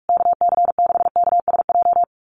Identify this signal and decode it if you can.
Morse code, heard as a keyed tone, about 44 wpm, 700 Hz. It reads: KC6KSJ
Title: USA Call Signs Repeated 3x 45wpm